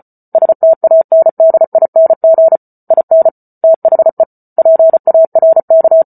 Morse code, heard as a keyed tone, about 34 wpm, 650 Hz. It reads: STANDING IN THE PARK